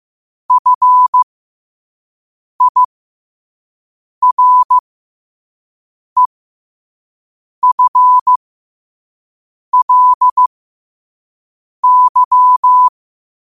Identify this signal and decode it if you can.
Morse code, heard as a keyed tone, about 15 wpm, spaced as for 5 wpm, 1000 Hz. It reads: FIREFLY
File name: qotdfiles / 5wpm